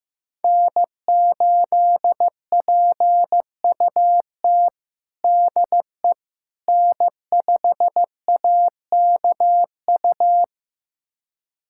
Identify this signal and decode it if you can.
Morse code, heard as a keyed tone, about 15 wpm, 700 Hz. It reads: N8PUT DE N5AKU
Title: USA Callsign DE Callsign - 5 Characters - 15wpm